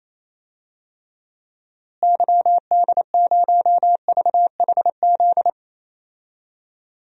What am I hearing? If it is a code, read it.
Morse code, heard as a keyed tone, about 27 wpm, 700 Hz. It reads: YD0VHZ
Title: DXCC-100 Call Signs - 6 Characters - Repeated 2x - 28wpm